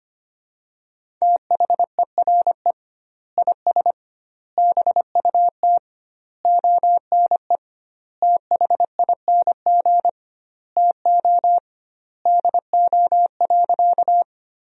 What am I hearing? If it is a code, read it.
Morse code, heard as a keyed tone, about 25 wpm, 700 Hz. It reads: THERE IS BUT ONE THING TO DO.
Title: 2x Word-Spacing - Sentences from Top 100 Words - 25wpm